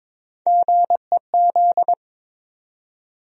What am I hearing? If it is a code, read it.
Morse code, heard as a keyed tone, about 22 wpm, 700 Hz. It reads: GEZ
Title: USA Call Sign Postfixes - 3 Letters 22wpm